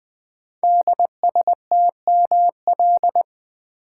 Morse code, heard as a keyed tone, about 20 wpm, 700 Hz. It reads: DSTML
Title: Code Groups - 5 Letters - 20wpm